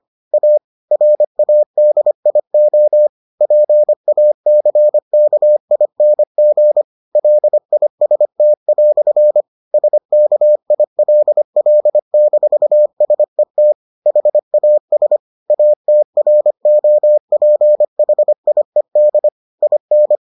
Morse code, heard as a keyed tone, about 25 wpm, 600 Hz. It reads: A RADIO PACKING LIST" SKILL-SET HAS ATROPHIED IN